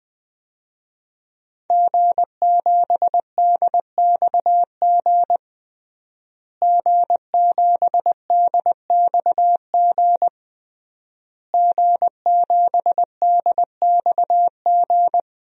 Morse code, heard as a keyed tone, about 20 wpm, 700 Hz. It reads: G7DXG G7DXG G7DXG
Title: DXCC-100 Call Signs - 5 Characters - Repeated 3x - 20wpm